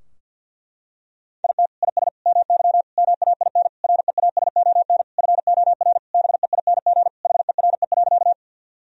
Morse code, heard as a keyed tone, about 52 wpm, 700 Hz. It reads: IT IS MY MAIN REASON FOR BEING HERE.
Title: Sentences from Top 1000 Words - Review Part 2 - 50wpm